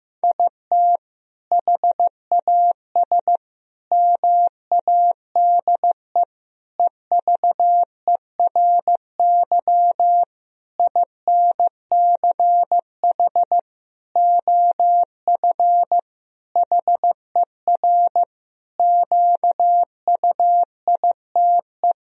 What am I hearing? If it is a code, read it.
Morse code, heard as a keyed tone, about 15 wpm, 700 Hz. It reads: IT HAS MADE EVERY INCH OF HER QUITE
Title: Sentences from Top 1000 Words - Review Part 3 - 15wpm